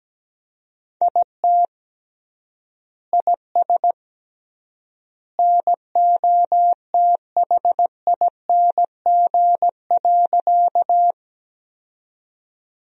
Morse code, heard as a keyed tone, about 17 wpm, 700 Hz. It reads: IT IS NOTHING.
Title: 3x Word-Spacing - Sentences from Top 500 Words - 17wpm